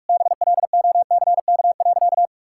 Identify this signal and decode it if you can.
Morse code, heard as a keyed tone, about 46 wpm, 700 Hz. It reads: BLOCK.
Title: Sentences from Top 1600 Words 45wpm